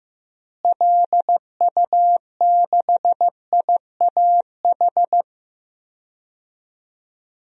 Morse code, heard as a keyed tone, about 15 wpm, 700 Hz. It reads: LU6IAH
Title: DXCC-100 Call Signs - 6 Characters - Repeated 2x - 15wpm